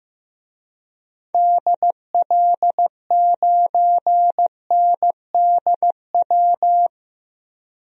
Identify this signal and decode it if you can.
Morse code, heard as a keyed tone, about 15 wpm, 700 Hz. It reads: DL9NDW
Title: DXCC-100 Call Signs - 6 Characters - 15wpm